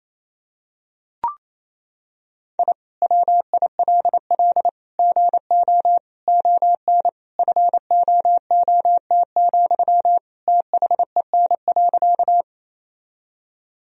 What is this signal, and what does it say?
Morse code, heard as a keyed tone, about 28 wpm, 700 Hz. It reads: I WILL GO ON FOOT, THEN.
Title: Sentences from Top 1000 Words - Review Part 3 - 28wpm